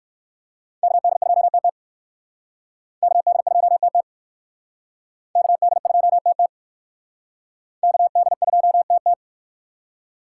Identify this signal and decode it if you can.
Morse code, heard as a keyed tone, about 59 wpm, 700 Hz. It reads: KD2TT KD2TT KD2TT KD2TT